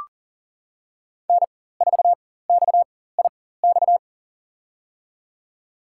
Morse code, heard as a keyed tone, about 40 wpm, spaced as for 19 wpm, 700 Hz. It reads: N4XIX
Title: USA Call Signs - 5 Characters - 4x Character Spacing - 40wpm